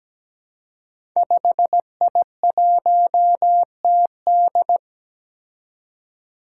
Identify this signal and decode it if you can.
Morse code, heard as a keyed tone, about 17 wpm, 700 Hz. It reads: 5I1TD
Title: Five Challenging Letter-Number Combinations 17wpm